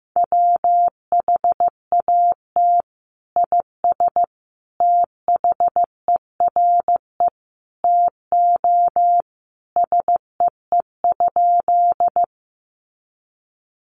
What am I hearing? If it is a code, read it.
Morse code, heard as a keyed tone, about 15 wpm, 700 Hz. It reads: WHAT IS THERE TO SEE?
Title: Sentences from Top 100 Words 15wpm